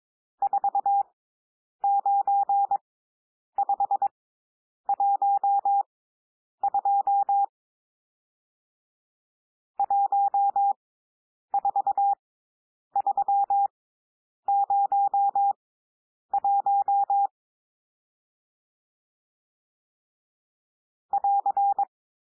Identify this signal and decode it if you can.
Morse code, heard as a keyed tone, about 22 wpm, spaced as for 9 wpm, 800 Hz. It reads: 49512 14301 <AR>
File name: cwmp3 / N1 SOUND NUMBERS 22 wpm